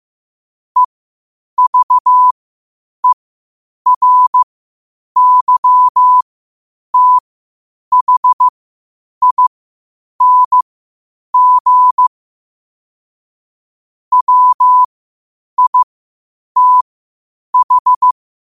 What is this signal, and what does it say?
Morse code, heard as a keyed tone, about 15 wpm, spaced as for 8 wpm, 1000 Hz. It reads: EVERYTHING WITH